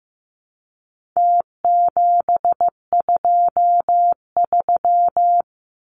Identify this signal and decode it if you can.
Morse code, heard as a keyed tone, about 15 wpm, 700 Hz. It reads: T723
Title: Numbers - Contest Style - 4-Digit 15wpm